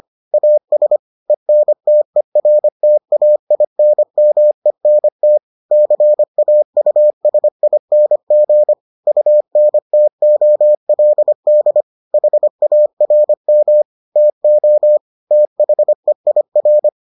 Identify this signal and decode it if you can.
Morse code, heard as a keyed tone, about 25 wpm, 600 Hz. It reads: AS ENTERTAINMENT CAUSING UNTOLD HARM TO THEIR